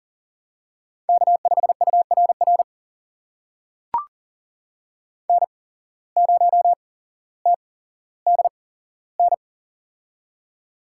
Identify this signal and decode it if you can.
Morse code, heard as a keyed tone, about 40 wpm, spaced as for 17 wpm, 700 Hz. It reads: K5URR N0TDN
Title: USA Call Signs - 8x Character Spacing - 40wpm